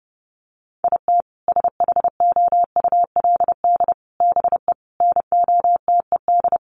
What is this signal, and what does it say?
Morse code, heard as a keyed tone, about 30 wpm, 700 Hz. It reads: IT SHOULD BE NOTED
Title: Sets of 5 Words 30wpm